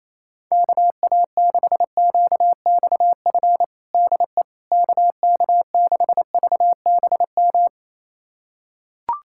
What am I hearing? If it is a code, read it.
Morse code, heard as a keyed tone, about 28 wpm, 700 Hz. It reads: KA6QXF DE KK6VBM
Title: USA Callsign DE Callsign - 6 Characters - 28wpm